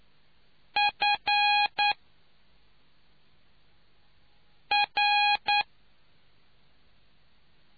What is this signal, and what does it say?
Morse code, heard as a keyed tone, about 9 wpm, spaced as for 3 wpm, 820 Hz. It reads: FR